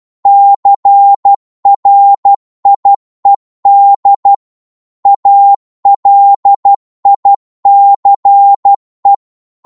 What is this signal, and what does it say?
Morse code, heard as a keyed tone, about 12 wpm, 800 Hz. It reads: CRIED ALICE
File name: Morse / psg-12wpm